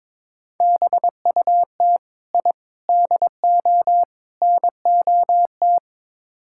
Morse code, heard as a keyed tone, about 22 wpm, 700 Hz. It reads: BUT I DO NOT
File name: practice / 22wpm / Build A Phrase - 5-Word Phrases - 22wpm